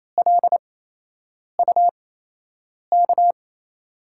Morse code, heard as a keyed tone, about 28 wpm, spaced as for 8 wpm, 700 Hz. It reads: LUK
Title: USA Call Signs - 6 Characters - 8x Character Spacing - 28wpm